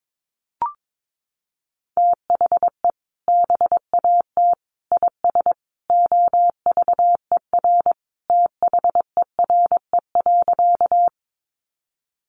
Morse code, heard as a keyed tone, about 22 wpm, 700 Hz. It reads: THE BAT IS OVER THERE.
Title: Sentences from Top 1500 Words 22wpm